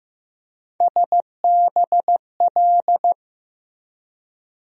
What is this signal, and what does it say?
Morse code, heard as a keyed tone, about 16 wpm, 700 Hz. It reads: SBL
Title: USA Call Sign Postfixes - 3 Letters 15wpm